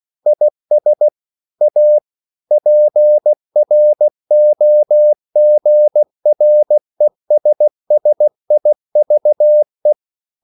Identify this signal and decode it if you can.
Morse code, heard as a keyed tone, about 16 wpm, 600 Hz. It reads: IS A PROGRESSIVE
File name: CW / Practice / EnglishQuotations0216